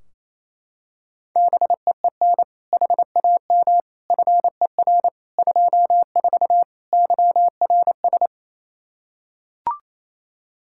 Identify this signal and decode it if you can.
Morse code, heard as a keyed tone, about 27 wpm, 700 Hz. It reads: BEEN HAM FER 24 YRS
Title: Practice QSO - Exchange 3 - More Info - 28wpm